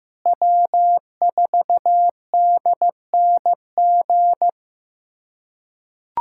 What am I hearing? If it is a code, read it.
Morse code, heard as a keyed tone, about 15 wpm, 700 Hz. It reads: W4DNG
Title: USA Callsign DE Callsign - 5 Characters - 15wpm